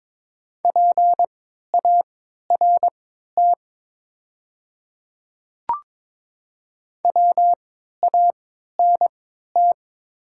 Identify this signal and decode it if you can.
Morse code, heard as a keyed tone, about 22 wpm, spaced as for 13 wpm, 700 Hz. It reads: PART WANT
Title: 3x Character Spacing - Top 300 Words - 22wpm